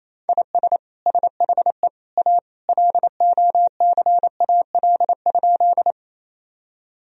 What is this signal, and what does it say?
Morse code, heard as a keyed tone, about 28 wpm, 700 Hz. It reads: IS SHE A LOCAL?
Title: Sentences from Top 1000 Words - Review Part 3 - 28wpm